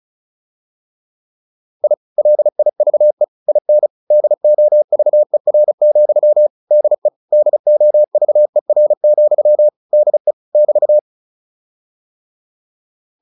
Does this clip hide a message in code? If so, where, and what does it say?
Yes: Morse code from 1.8 to 11.0 s, heard as a keyed tone, about 34 wpm, 600 Hz. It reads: I LIVE IN DOVER, DE DOVER, DE <BT>